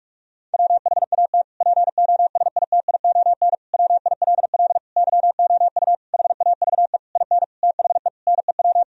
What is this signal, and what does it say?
Morse code, heard as a keyed tone, about 46 wpm, 700 Hz. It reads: WHAT POSITION WILL YOU HAVE IN THE NEW
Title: Sentences from Top 1000 Words - Review Part 3 - 45wpm